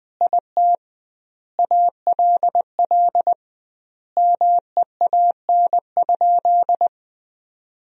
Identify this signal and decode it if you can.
Morse code, heard as a keyed tone, about 20 wpm, 700 Hz. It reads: IT ALL MEAN?
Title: 2x Word-Spacing - Sentences from Top 200 Words - 20wpm